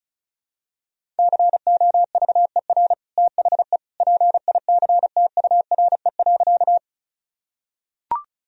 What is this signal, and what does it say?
Morse code, heard as a keyed tone, about 36 wpm, 700 Hz. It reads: COVER THE PICTURE.
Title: Sentences from Top 1000 Words - Review Part 1 - 35wpm